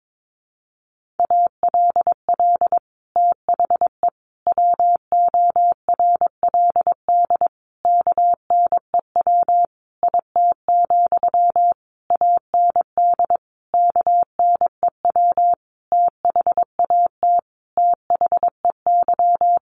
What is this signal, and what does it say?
Morse code, heard as a keyed tone, about 22 wpm, 700 Hz. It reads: ALL THE WORLD KNEW IT, AND KNEW THAT THEY
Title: Sentences from Top 1200 Words 22wpm